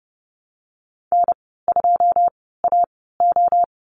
Morse code, heard as a keyed tone, about 30 wpm, spaced as for 17 wpm, 700 Hz. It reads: N2AO
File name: Lesson 14g - Number 2 - Call Signs 30wpm